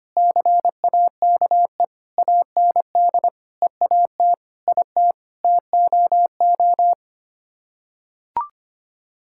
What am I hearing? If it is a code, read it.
Morse code, heard as a keyed tone, about 25 wpm, 700 Hz. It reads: CAKE AND EAT IT TOO
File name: Sets of 9 Words - 25wpm